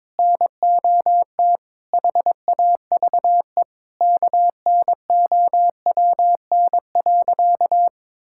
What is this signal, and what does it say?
Morse code, heard as a keyed tone, about 22 wpm, 700 Hz. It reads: NOT HAVE KNOWN.
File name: practice / 22wpm / Sentences with 3-Word Phrases - 22wpm